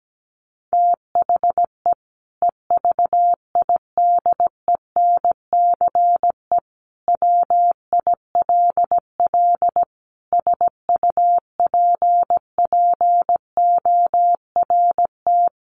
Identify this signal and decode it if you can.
Morse code, heard as a keyed tone, about 17 wpm, 700 Hz. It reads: THE EVIDENCE WILL SUPPORT
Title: Sentences from Top 1300 Words 17wpm